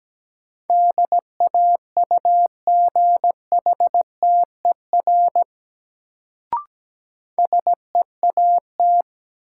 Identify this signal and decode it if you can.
Morse code, heard as a keyed tone, about 17 wpm, 700 Hz. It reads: DAUGHTER SEAT
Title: Sentences from Top 1500 Words - New Word Drill Alt - 17wpm